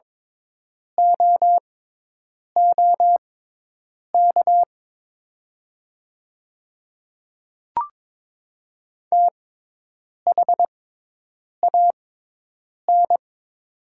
Morse code, heard as a keyed tone, about 22 wpm, spaced as for 8 wpm, 690 Hz. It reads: OOK THAN
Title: 6x Character Spacing - Top 300 Words - 22wpm